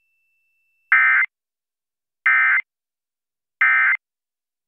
Emergency Alert System (SAME) tones, heard as harsh buzzing bursts. It is heard from 0.9 to 4.0 s.